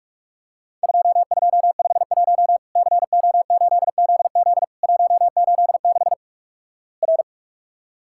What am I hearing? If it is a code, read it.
Morse code, heard as a keyed tone, about 46 wpm, 700 Hz. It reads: 2251 CO8ZZ 186 R
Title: Worked All Europe DX Contest - QTC Exchange - 45wpm